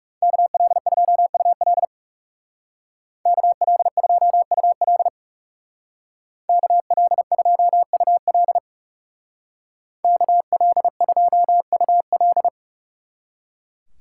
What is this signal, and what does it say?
Morse code, heard as a keyed tone, about 36 wpm, 700 Hz. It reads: KL2UL KL2UL KL2UL KL2UL
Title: Speed Racing - USA Call Signs 30wpm